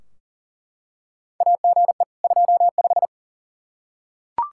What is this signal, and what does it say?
Morse code, heard as a keyed tone, about 40 wpm, 700 Hz. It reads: AGE 25
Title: Practice QSO - Exchange 3 - More Info - 40wpm